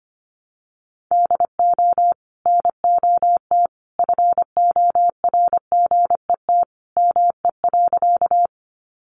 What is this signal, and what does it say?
Morse code, heard as a keyed tone, about 25 wpm, 700 Hz. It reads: DO NOT FORGET ME.
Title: Sentences from Top 1200 Words 25wpm